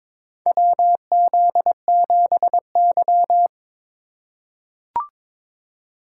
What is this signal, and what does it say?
Morse code, heard as a keyed tone, about 22 wpm, 700 Hz. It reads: WZ7Y
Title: USA Call Signs - 4 Characters - 6x Character Spacing - 22wpm